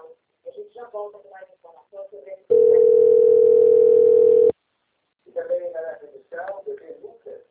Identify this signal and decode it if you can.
Ringback tone (North America), 440 + 480 Hz, with speech.
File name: out- - - - - .4977